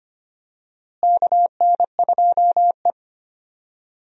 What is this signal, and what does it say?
Morse code, heard as a keyed tone, about 25 wpm, 700 Hz. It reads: KN2E
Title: USA Call Signs - 4 Characters - Repeated 3x - 25wpm